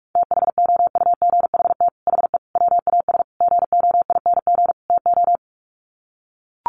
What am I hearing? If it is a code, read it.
Morse code, heard as a keyed tone, about 44 wpm, 700 Hz. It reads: THOUGHT HE WAS GOING TO
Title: Sets of 5 Words Encore2 45wpm